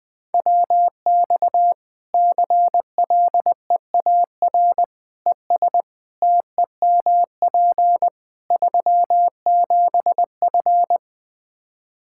Morse code, heard as a keyed tone, about 20 wpm, 700 Hz. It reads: WX CLEAR ES TEMP 37F
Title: K1USN SST QSO Practice 20wpm